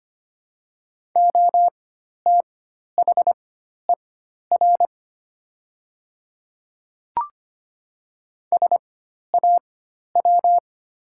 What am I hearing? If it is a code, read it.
Morse code, heard as a keyed tone, about 25 wpm, spaced as for 12 wpm, 700 Hz. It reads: OTHER SAW
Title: 4x Character Spacing - Top 300 Words - 25wpm